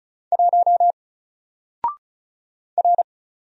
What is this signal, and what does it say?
Morse code, heard as a keyed tone, about 36 wpm, 700 Hz. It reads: 1 R
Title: Single Letter-Number 35wpm